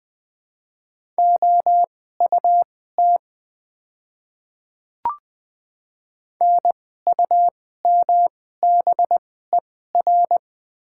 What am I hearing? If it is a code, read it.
Morse code, heard as a keyed tone, about 20 wpm, spaced as for 14 wpm, 700 Hz. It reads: OUT NUMBER
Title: 2x Character Spacing - Top 100 Words - 20wpm